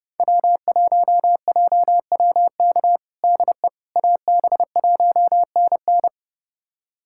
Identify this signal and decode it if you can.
Morse code, heard as a keyed tone, about 30 wpm, 700 Hz. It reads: W1JWK DE AB1NN
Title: USA Callsign DE Callsign - 5 Characters - 30wpm